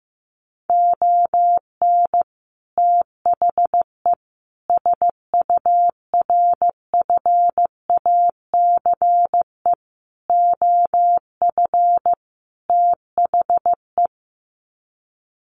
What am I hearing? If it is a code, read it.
Morse code, heard as a keyed tone, about 15 wpm, 700 Hz. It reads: ON THE SURFACE OF THE
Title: Sets of 5 Words Encore 15wpm